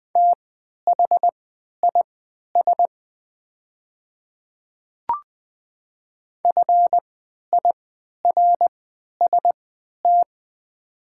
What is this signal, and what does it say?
Morse code, heard as a keyed tone, about 20 wpm, spaced as for 11 wpm, 700 Hz. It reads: THIS FIRST
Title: 3x Character Spacing - Top 100 Words - 20wpm